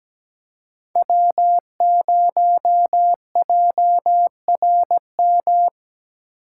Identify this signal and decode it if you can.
Morse code, heard as a keyed tone, about 17 wpm, 700 Hz. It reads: W0JRM
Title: USA Call Signs - 4x Character Spacing - 17wpm